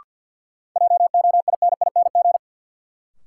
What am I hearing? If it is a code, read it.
Morse code, heard as a keyed tone, about 48 wpm, 700 Hz. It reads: JOINING
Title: Sets of 5 Words - New Word Drill - 50wpm